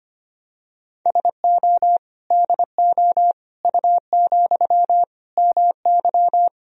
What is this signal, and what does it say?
Morse code, heard as a keyed tone, about 25 wpm, 700 Hz. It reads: SO DO U, MY
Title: Sentences from Top 400 Words and Abbreviations 25wpm